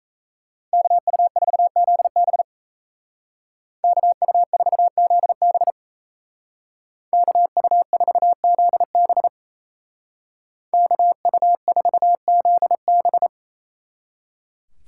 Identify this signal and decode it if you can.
Morse code, heard as a keyed tone, about 36 wpm, 700 Hz. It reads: KU4ZB KU4ZB KU4ZB KU4ZB